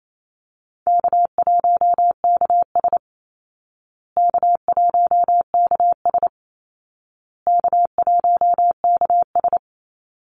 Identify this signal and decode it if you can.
Morse code, heard as a keyed tone, about 28 wpm, 700 Hz. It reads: K1KS K1KS K1KS
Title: Easy Call Signs Repeated 3x - 28wpm